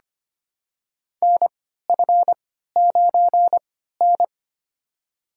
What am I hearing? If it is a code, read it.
Morse code, heard as a keyed tone, about 25 wpm, spaced as for 14 wpm, 700 Hz. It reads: NF9N
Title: USA Call Signs - 4 Characters - 3x Character Spacing - 25wpm